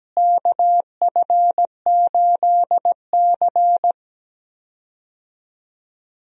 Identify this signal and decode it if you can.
Morse code, heard as a keyed tone, about 17 wpm, 700 Hz. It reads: KF8C
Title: USA Call Signs - 4 Characters - Repeated 3x - 17wpm